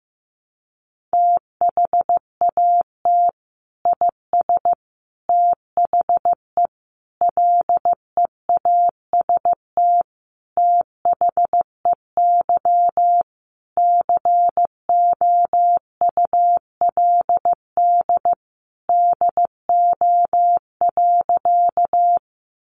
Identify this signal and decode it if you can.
Morse code, heard as a keyed tone, about 15 wpm, 700 Hz. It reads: THAT IS THE LEAST THEY COULD DO.
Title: Sentences from Top 700 Words 15wpm